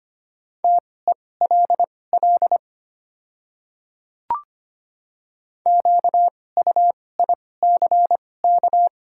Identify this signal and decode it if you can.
Morse code, heard as a keyed tone, about 25 wpm, spaced as for 18 wpm, 690 Hz. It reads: TELL QUICK
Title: 2x Character Spacing - Top 500 Words - 25wpm